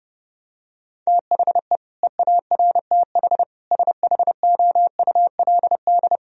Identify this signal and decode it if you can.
Morse code, heard as a keyed tone, about 30 wpm, 700 Hz. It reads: THE EARTH SHOULD
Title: Sentences from Top 1000 Words - Review Part 3 - 30wpm